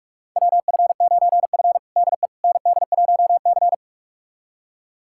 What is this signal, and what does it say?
Morse code, heard as a keyed tone, about 46 wpm, 700 Hz. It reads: WF9F DE ND1C
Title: USA Callsign DE Callsign - 4 Characters - 45wpm